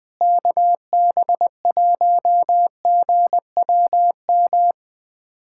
Morse code, heard as a keyed tone, about 20 wpm, 700 Hz. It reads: KB1GWM
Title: USA Callsign DE Callsign - 6 Characters - 20wpm